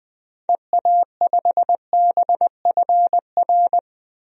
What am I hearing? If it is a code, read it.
Morse code, heard as a keyed tone, about 20 wpm, 700 Hz. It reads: EA5BFR
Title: DXCC-100 Call Signs - 6 Characters - Repeated 3x - 20wpm